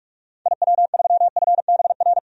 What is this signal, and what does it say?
Morse code, heard as a keyed tone, about 41 wpm, 700 Hz. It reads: IW3FBR
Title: DXCC-100 Call Signs - 6 Characters - 45wpm